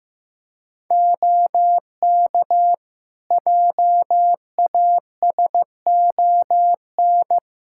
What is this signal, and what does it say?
Morse code, heard as a keyed tone, about 15 wpm, 700 Hz. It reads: OK JASON
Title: Practice QSO - Exchange 4 - Ending - 15wpm